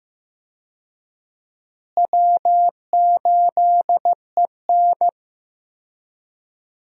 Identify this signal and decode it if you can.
Morse code, heard as a keyed tone, about 15 wpm, 700 Hz. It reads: W8EN
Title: USA Call Signs - 4 Characters - Repeated 3x - 15wpm